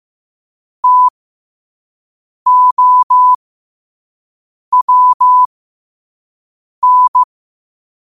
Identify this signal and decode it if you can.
Morse code, heard as a keyed tone, about 15 wpm, spaced as for 5 wpm, 1000 Hz. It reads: TOWN
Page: New quotes every day in morse code at 5 Words per minute.